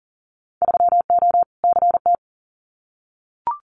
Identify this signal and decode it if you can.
Morse code, heard as a keyed tone, about 42 wpm, 700 Hz. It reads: 3O CT
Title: Winter Field Day Exchanges Encore 40wpm